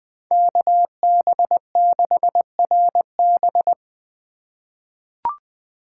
Morse code, heard as a keyed tone, about 20 wpm, 700 Hz. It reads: KB6RB
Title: USA Call Signs - 5 Characters - 8x Character Spacing - 20wpm